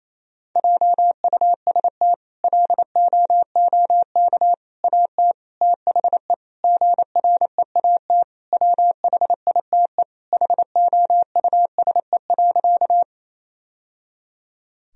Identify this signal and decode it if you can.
Morse code, heard as a keyed tone, about 28 wpm, 700 Hz. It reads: JUST LOOK AT THE GREAT WHITE HOUSE.
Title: Sentences from Top 1000 Words - Review Part 1 - 28wpm